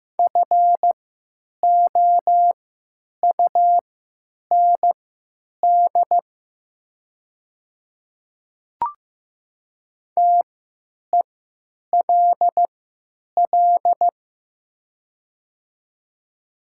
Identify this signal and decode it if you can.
Morse code, heard as a keyed tone, about 15 wpm, spaced as for 9 wpm, 700 Hz. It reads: FOUND TELL
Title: 3x Character Spacing - Top 300 Words - 15wpm